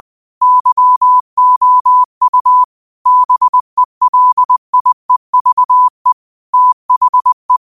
Morse code, heard as a keyed tone, about 20 wpm, 1000 Hz. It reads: YOU BELIEVE THE
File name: qotdfiles / 20wpm